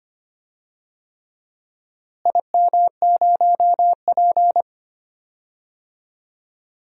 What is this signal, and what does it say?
Morse code, heard as a keyed tone, about 25 wpm, 700 Hz. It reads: IM0P
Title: DXCC-100 Call Signs - 4 Characters - Repeated 3x - 25wpm